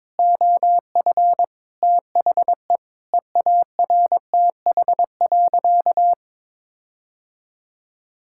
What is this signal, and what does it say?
Morse code, heard as a keyed tone, about 22 wpm, 700 Hz. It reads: OF THE EARTH.
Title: Sentences from Top 1600 Words 22wpm